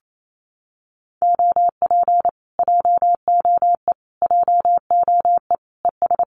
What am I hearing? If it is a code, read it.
Morse code, heard as a keyed tone, about 28 wpm, 700 Hz. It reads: OP JOE JOE ES
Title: Common QSO Phrases Encore 28wpm